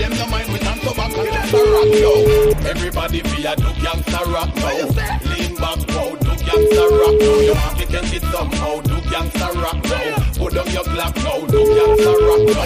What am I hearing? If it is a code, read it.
Ringback tone (Europe, 425 Hz), with noise.